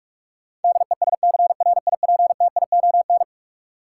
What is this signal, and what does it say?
Morse code, heard as a keyed tone, about 41 wpm, 700 Hz. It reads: DESCRIPTION
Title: Sentences from Top 1300 Words - New Word Drill - 45wpm